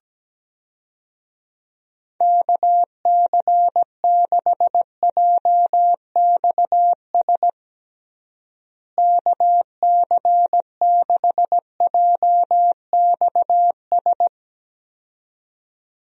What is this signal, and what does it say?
Morse code, heard as a keyed tone, about 17 wpm, 700 Hz. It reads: KC6JXS KC6JXS
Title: USA Call Signs - 6 Characters - Repeated 2x - 17wpm